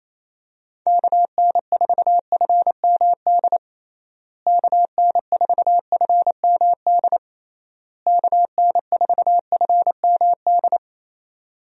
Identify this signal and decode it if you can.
Morse code, heard as a keyed tone, about 27 wpm, 700 Hz. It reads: KN4FMD KN4FMD KN4FMD
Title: USA Call Signs - 6 Characters - Repeated 3x - 28wpm